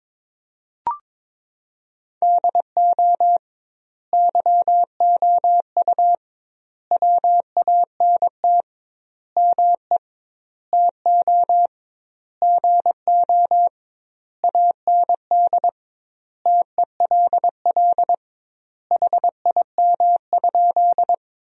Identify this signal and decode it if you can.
Morse code, heard as a keyed tone, about 22 wpm, 700 Hz. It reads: DO YOU WANT ME TO GO AND TELL HIM?
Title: 2x Word-Spacing - Sentences from Top 200 Words - 22wpm